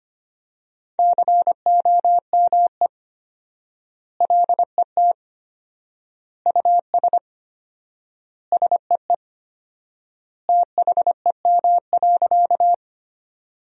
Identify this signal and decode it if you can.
Morse code, heard as a keyed tone, about 25 wpm, 700 Hz. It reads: COME LET US SEE THEM.
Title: 4x Word-Spacing - Sentences from Top 300 Words - 25wpm